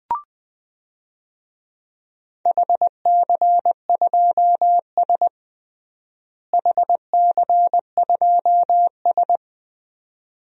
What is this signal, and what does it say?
Morse code, heard as a keyed tone, about 21 wpm, 700 Hz. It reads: HC2S HC2S
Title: DXCC-100 Call Signs - 4 Characters - Repeated 2x - 20wpm